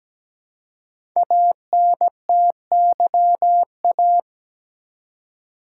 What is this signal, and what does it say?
Morse code, heard as a keyed tone, about 17 wpm, 700 Hz. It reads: ANTYA
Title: Code Groups - 5 Letter:Numbers - 17wpm